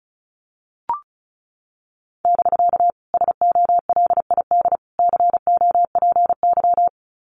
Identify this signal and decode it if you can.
Morse code, heard as a keyed tone, about 34 wpm, 700 Hz. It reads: <BK> SOLID COPY
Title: Common QSO Phrases Encore 35wpm